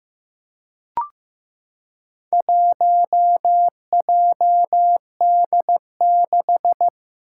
Morse code, heard as a keyed tone, about 15 wpm, 700 Hz. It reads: 1JD6
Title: Four Challenging Letter-Number Combinations 15wpm